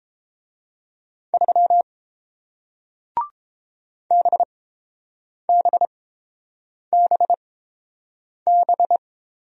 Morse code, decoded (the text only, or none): 3 BBBB